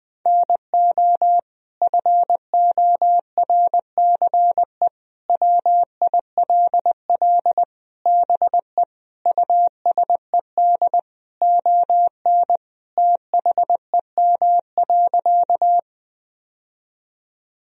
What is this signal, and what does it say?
Morse code, heard as a keyed tone, about 20 wpm, 700 Hz. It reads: NO FORCE WILL BE USED ON THEM.
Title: Sentences from Top 1000 Words - Review Part 3 - 20wpm